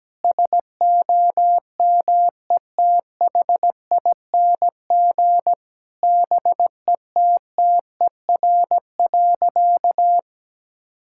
Morse code, heard as a keyed tone, about 17 wpm, 700 Hz. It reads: SOMETHING BETTER.
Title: Sentences from Top 1000 Words - Review Part 6 - 17wpm